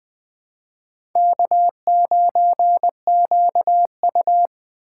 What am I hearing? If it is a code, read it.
Morse code, heard as a keyed tone, about 24 wpm, 700 Hz. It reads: K9QU
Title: Speed Racing - USA 1x2 Call Signs 20wpm